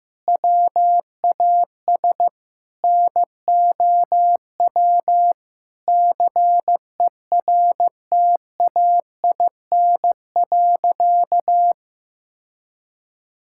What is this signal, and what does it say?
Morse code, heard as a keyed tone, about 16 wpm, 700 Hz. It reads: WAS NOW CERTAIN.
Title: Sentences from Top 1000 Words - Review Part 5 - 15wpm